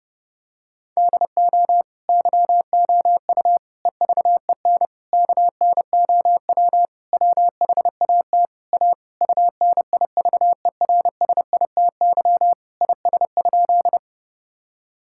Morse code, heard as a keyed tone, about 30 wpm, 700 Hz. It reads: DO YOU EVEN KNOW WHAT A UNIVERSITY IS?
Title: Sentences from Top 1000 Words - Review Part 5 - 30wpm